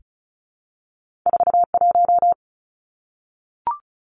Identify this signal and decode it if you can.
Morse code, heard as a keyed tone, about 36 wpm, 700 Hz. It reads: 41
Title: Numbers - 2-Digit 35wpm